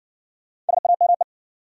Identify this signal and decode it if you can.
Morse code, heard as a keyed tone, about 59 wpm, 700 Hz. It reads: SAME